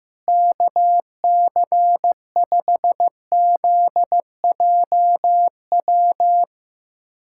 Morse code, heard as a keyed tone, about 15 wpm, 700 Hz. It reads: KC5ZJW
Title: USA Callsign DE Callsign 15wpm